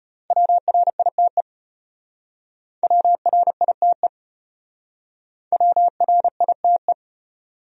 Morse code, heard as a keyed tone, about 33 wpm, 700 Hz. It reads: WRITE WRITE WRITE